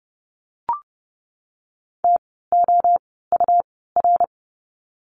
Morse code, heard as a keyed tone, about 30 wpm, spaced as for 17 wpm, 700 Hz. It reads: TOUR
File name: Lesson 17f - Letter u - Words 30wpm